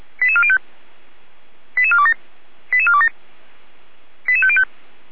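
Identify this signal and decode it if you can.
Five-tone selective call (ZVEI), heard as quick runs of five tones, tones of about 70 ms. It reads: 88405 88427 88427 88505
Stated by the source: ZVEI five-tone selective call received via SDR, address 88447, from the feuerwehr-orlamuende.org fire service audio archive.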